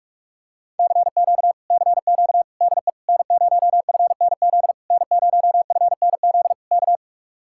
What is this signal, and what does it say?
Morse code, heard as a keyed tone, about 44 wpm, 700 Hz. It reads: CQ CQ DE N0FNZ N0FNZ K